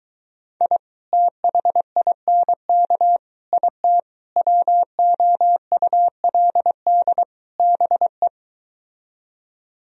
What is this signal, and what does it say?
Morse code, heard as a keyed tone, about 23 wpm, 700 Hz. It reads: I THINK IT WOULD BE